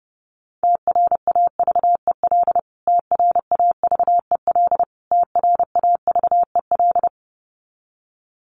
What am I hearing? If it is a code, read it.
Morse code, heard as a keyed tone, about 30 wpm, 700 Hz. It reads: TRAVEL TRAVEL TRAVEL
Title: Top 500 Words Repeated 30wpm